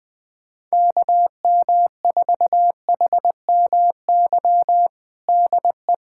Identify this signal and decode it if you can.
Morse code, heard as a keyed tone, about 20 wpm, 700 Hz. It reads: KM4HMY DE
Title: USA Callsign DE Callsign - 6 Characters - 20wpm